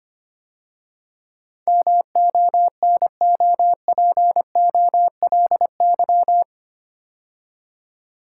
Morse code, heard as a keyed tone, about 25 wpm, 700 Hz. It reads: MONOPOLY
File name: 8 Letter Words - First 3 Letters Given - 25wpm